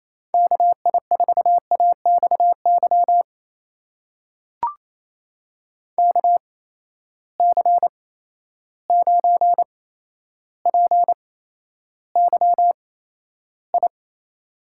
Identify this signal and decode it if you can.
Morse code, heard as a keyed tone, about 28 wpm, spaced as for 12 wpm, 700 Hz. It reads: KI4AXY KC9PYI
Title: USA Call Signs - 6 Characters - 8x Character Spacing - 28wpm